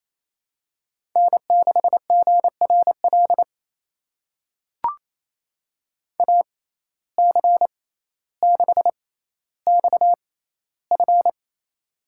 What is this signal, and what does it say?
Morse code, heard as a keyed tone, about 28 wpm, spaced as for 14 wpm, 700 Hz. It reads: N6GRL AC6XF
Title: USA Call Signs - 5 Characters - 6x Character Spacing - 28wpm